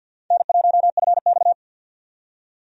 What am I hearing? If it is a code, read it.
Morse code, heard as a keyed tone, about 50 wpm, 700 Hz. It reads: N1FX
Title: USA Call Signs - 4 Characters - 50wpm